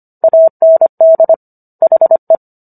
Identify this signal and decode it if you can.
Morse code, heard as a keyed tone, about 25 wpm, 650 Hz. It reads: AND HE